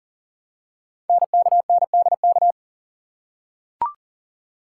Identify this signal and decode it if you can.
Morse code, heard as a keyed tone, about 43 wpm, 700 Hz. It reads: NKNDK
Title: Code Groups - 5 Letters - 40wpm